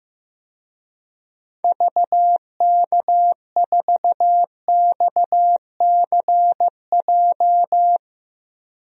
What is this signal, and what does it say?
Morse code, heard as a keyed tone, about 16 wpm, 700 Hz. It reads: VK4XCJ
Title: DXCC-100 Call Signs - 6 Characters - 15wpm